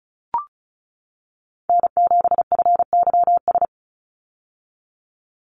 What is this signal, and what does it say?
Morse code, heard as a keyed tone, about 34 wpm, 700 Hz. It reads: N7FYS
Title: USA Call Signs 35wpm